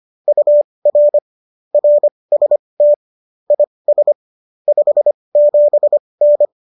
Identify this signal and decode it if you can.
Morse code, heard as a keyed tone, about 25 wpm, spaced as for 20 wpm, 590 Hz. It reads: UR RST IS 57N
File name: qso201 20